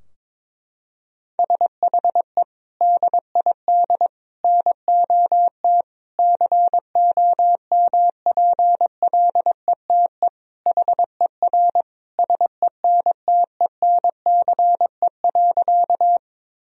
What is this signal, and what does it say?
Morse code, heard as a keyed tone, about 22 wpm, 700 Hz. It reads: SHE DID NOT COMPLETE HER SENTENCE.
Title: Sentences from Top 1000 Words - Review Part 2 - 22wpm